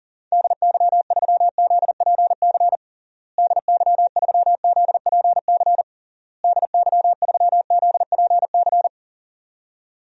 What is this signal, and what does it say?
Morse code, heard as a keyed tone, about 37 wpm, 700 Hz. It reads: DY3ZPC DY3ZPC DY3ZPC
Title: DXCC-100 Call Signs - 6 Characters - Repeated 3x - 40wpm